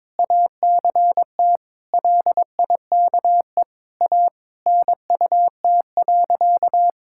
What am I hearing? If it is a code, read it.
Morse code, heard as a keyed tone, about 21 wpm, 700 Hz. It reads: ACT LIKE A NUT.
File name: All-Age Appropriate Jokes - 22wpm